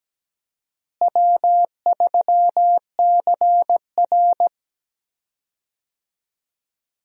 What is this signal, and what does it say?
Morse code, heard as a keyed tone, about 17 wpm, 700 Hz. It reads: W3CR
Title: USA Call Signs - 4 Characters - Repeated 2x - 17wpm